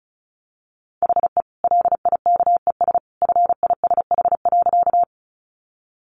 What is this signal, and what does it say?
Morse code, heard as a keyed tone, about 34 wpm, 700 Hz. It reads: HE LIKES FISH.
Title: Additional Easy Phrases 35wpm